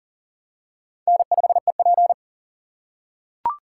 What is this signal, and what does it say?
Morse code, heard as a keyed tone, about 40 wpm, 700 Hz. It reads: N5EP
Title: USA Call Signs - 4 Characters - 6x Character Spacing - 40wpm